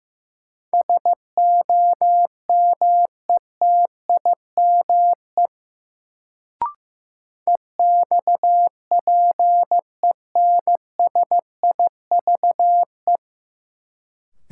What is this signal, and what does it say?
Morse code, heard as a keyed tone, about 15 wpm, 700 Hz. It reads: SOMETIME EXPENSIVE
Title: Sentences from Top 1500 Words - New Word Drill Alt - 15wpm